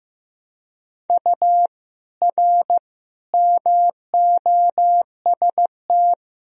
Morse code, heard as a keyed tone, about 16 wpm, 700 Hz. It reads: U R MOST
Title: Sentences from Top 200 Words and Abbreviations 15wpm